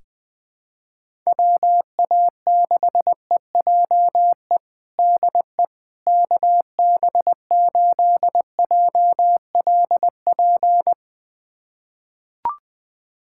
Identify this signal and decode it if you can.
Morse code, heard as a keyed tone, about 20 wpm, 700 Hz. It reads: WA6EJE DE KB8JLP
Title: USA Callsign DE Callsign - 6 Characters - 20wpm